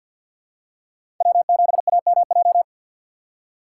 Morse code, heard as a keyed tone, about 50 wpm, 700 Hz. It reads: W7AMJ
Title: USA Call Signs - 5 Characters - 6x Character Spacing - 50wpm